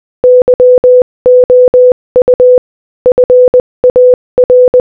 Morse code, heard as a keyed tone, about 20 wpm, 500 Hz. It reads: YOU FAR
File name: Morse-Code-Audio-Speed-20-13